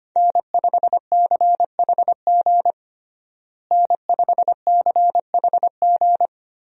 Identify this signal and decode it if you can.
Morse code, heard as a keyed tone, about 25 wpm, 700 Hz. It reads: N5CHG N5CHG
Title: USA Call Signs - 5 Characters - Repeated 2x - 25wpm